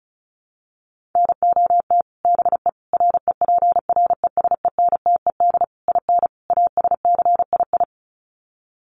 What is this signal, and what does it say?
Morse code, heard as a keyed tone, about 34 wpm, 700 Hz. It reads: NOT BE REPRESENTED IN ASCII
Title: Sets of 5 Words 35wpm